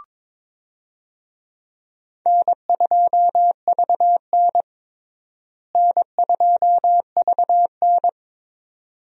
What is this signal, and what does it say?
Morse code, heard as a keyed tone, about 22 wpm, 700 Hz. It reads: N2VN N2VN
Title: USA Call Signs - 4 Characters - Repeated 2x - 22wpm